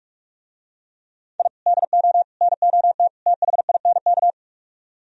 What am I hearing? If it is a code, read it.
Morse code, heard as a keyed tone, about 44 wpm, 700 Hz. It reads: I DO NOT THINK